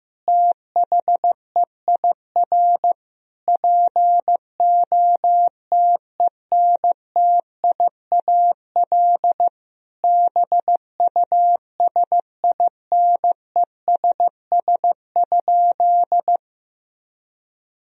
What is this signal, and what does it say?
Morse code, heard as a keyed tone, about 16 wpm, 700 Hz. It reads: THEIR POTENTIAL BUSINESS?
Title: Sentences from Top 1000 Words - Review Part 4 - 15wpm